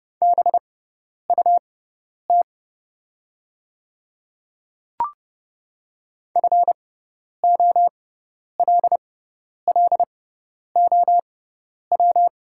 Morse code, heard as a keyed tone, about 30 wpm, spaced as for 10 wpm, 700 Hz. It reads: BUT FOLLOW
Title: 6x Character Spacing - Top 300 Words - 30wpm